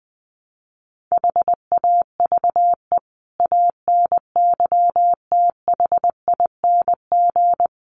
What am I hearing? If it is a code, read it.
Morse code, heard as a keyed tone, about 20 wpm, 700 Hz. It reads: HAVE ANYTHING
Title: Sets of 5 Words 20wpm